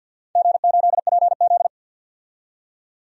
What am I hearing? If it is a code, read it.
Morse code, heard as a keyed tone, about 53 wpm, 700 Hz. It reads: G8PZ
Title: DXCC-100 Call Signs - 4 Characters - 50wpm